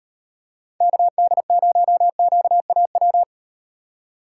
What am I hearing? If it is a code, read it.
Morse code, heard as a keyed tone, about 36 wpm, 700 Hz. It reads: KD0QAW